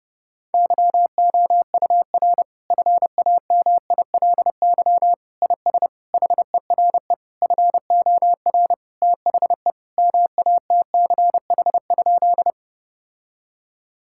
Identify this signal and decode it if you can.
Morse code, heard as a keyed tone, about 30 wpm, 700 Hz. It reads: YOUR FAMILY IS HERE FOR THE MATCH?
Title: Sentences from Top 1600 Words 30wpm